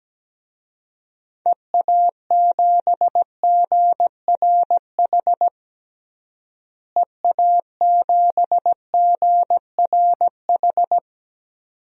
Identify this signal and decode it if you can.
Morse code, heard as a keyed tone, about 17 wpm, 700 Hz. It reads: EA7GRH EA7GRH
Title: DXCC-100 Call Signs - 6 Characters - Repeated 2x - 17wpm